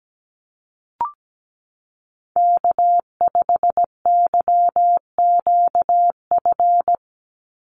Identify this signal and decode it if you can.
Morse code, heard as a keyed tone, about 17 wpm, 700 Hz. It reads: K5YQF
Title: Call Signs - 17wpm